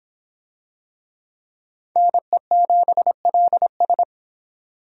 Morse code, heard as a keyed tone, about 25 wpm, 700 Hz. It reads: NE7LS